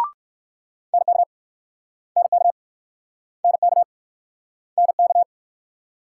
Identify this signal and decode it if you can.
Morse code, heard as a keyed tone, about 56 wpm, 700 Hz. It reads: NK NK NK NK
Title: Speed-Racing Multi-letter Phonemes - 45wpm